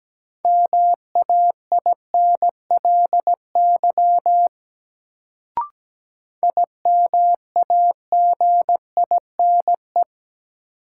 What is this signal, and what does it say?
Morse code, heard as a keyed tone, about 17 wpm, 700 Hz. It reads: MAINLY IMAGINE
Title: Sentences from Top 1400 Words - New Word Drill Alt - 17wpm